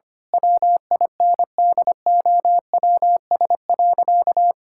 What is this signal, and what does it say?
Morse code, heard as a keyed tone, about 25 wpm, 700 Hz. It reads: WINDOWS.
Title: Pangram Challenge v2 - 25wpm